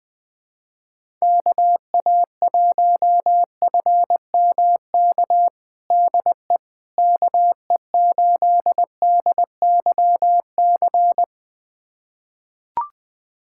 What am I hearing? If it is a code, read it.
Morse code, heard as a keyed tone, about 20 wpm, 700 Hz. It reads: KA1FMK DE KE8DYC
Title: USA Callsign DE Callsign - 6 Characters - 20wpm